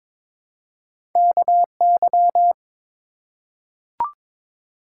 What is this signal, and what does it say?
Morse code, heard as a keyed tone, about 22 wpm, 700 Hz. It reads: KY
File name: ARRL Sections v2 - 22wpm